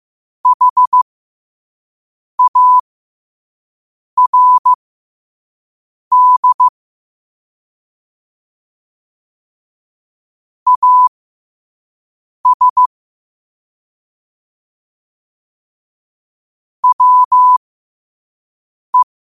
Morse code, heard as a keyed tone, about 15 wpm, spaced as for 5 wpm, 1000 Hz. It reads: HARD AS WE
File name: qotdfiles / 5wpm